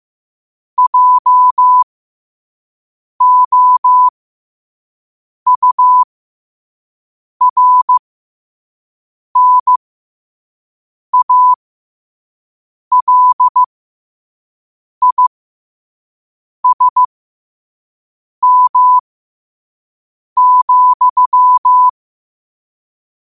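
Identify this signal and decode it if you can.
Morse code, heard as a keyed tone, about 15 wpm, spaced as for 5 wpm, 1000 Hz. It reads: JOURNALISM,
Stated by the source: Morse code quote of the day sent at 5 WPM.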